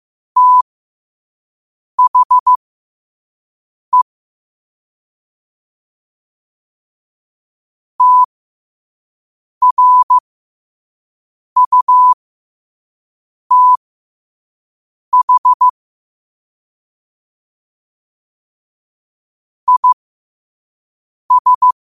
Morse code, heard as a keyed tone, about 15 wpm, spaced as for 5 wpm, 1000 Hz. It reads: THE TRUTH IS
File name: qotdfiles / 5wpm